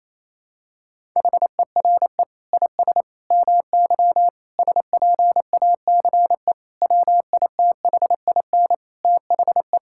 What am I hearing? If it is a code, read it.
Morse code, heard as a keyed tone, about 28 wpm, 700 Hz. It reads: HERE IS MY SPACE WITHIN THE
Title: Sentences from Top 1000 Words - Review Part 5 - 28wpm